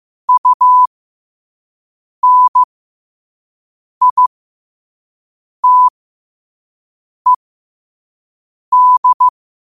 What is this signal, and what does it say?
Morse code, heard as a keyed tone, about 15 wpm, spaced as for 5 wpm, 1000 Hz. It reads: UNITED
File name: 2026-02-15-5wpm